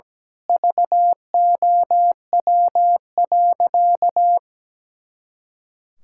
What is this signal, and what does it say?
Morse code, heard as a keyed tone, about 17 wpm, 700 Hz. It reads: VOW.
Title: Pangram Challenge v2 - 17wpm